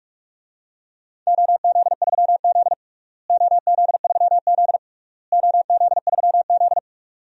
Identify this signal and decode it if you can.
Morse code, heard as a keyed tone, about 46 wpm, 700 Hz. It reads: OZ3Z OZ3Z OZ3Z
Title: DXCC-100 Call Signs - 4 Characters - Repeated 3x - 45wpm